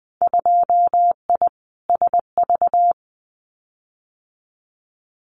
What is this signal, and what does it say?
Morse code, heard as a keyed tone, about 19 wpm, 700 Hz. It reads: 2I SV
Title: Winter Field Day Exchanges Encore 20wpm